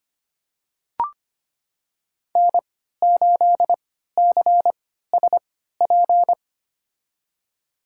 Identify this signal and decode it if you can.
Morse code, heard as a keyed tone, about 25 wpm, spaced as for 14 wpm, 700 Hz. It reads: N8CSP
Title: USA Call Signs - 5 Characters - 3x Character Spacing - 25wpm